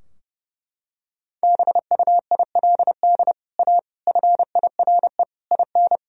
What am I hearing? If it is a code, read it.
Morse code, heard as a keyed tone, about 31 wpm, 700 Hz. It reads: BUILD A FIRE IN
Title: Sentences from Top 1000 Words - Review Part 4 - 30wpm